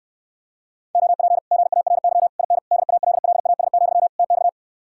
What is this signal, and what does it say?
Morse code, heard as a keyed tone, about 69 wpm, 700 Hz. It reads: OP GARY IN DALLAS, TX